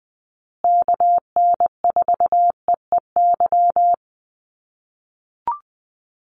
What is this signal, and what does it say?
Morse code, heard as a keyed tone, about 20 wpm, 700 Hz. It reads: KN4EEY
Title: USA Call Signs Repeated 3x 20wpm